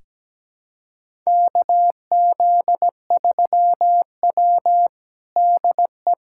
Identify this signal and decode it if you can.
Morse code, heard as a keyed tone, about 17 wpm, 700 Hz. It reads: KZ3W DE
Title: USA Callsign DE Callsign - 4 Characters - 17wpm